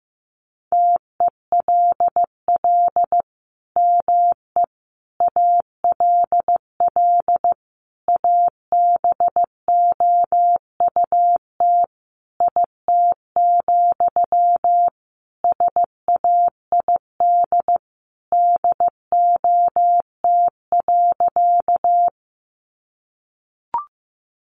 Morse code, heard as a keyed tone, about 15 wpm, 700 Hz. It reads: TELL ME ALL ABOUT IT, SAID DOT.
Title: Sentences from Top 1500 Words 15wpm